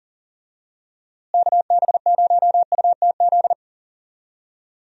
Morse code, heard as a keyed tone, about 40 wpm, 700 Hz. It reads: KB0UTZ
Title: USA Call Signs - 6 Characters - Repeated 3x - 40wpm